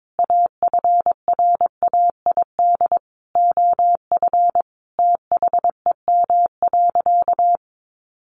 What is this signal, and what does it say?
Morse code, heard as a keyed tone, about 22 wpm, 700 Hz. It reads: AFRAID OF THEM.
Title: Sentences from Top 1500 Words 22wpm